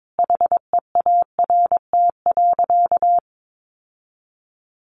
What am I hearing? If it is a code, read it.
Morse code, heard as a keyed tone, about 22 wpm, 700 Hz. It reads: HEART.
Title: Sentences from Top 1500 Words 22wpm